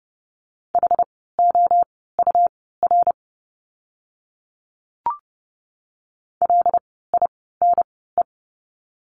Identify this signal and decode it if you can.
Morse code, heard as a keyed tone, about 30 wpm, spaced as for 17 wpm, 700 Hz. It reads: HOUR LINE
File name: course / 30:10wpm / 115 Lesson 19f - Letter w - Words 30wpm